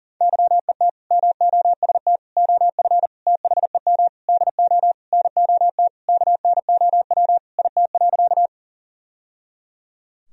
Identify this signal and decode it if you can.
Morse code, heard as a keyed tone, about 37 wpm, 700 Hz. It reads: YET MOST OF THEM DO NOT KNOW IT.
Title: Sentences from Top 1000 Words - Review Part 1 - 40wpm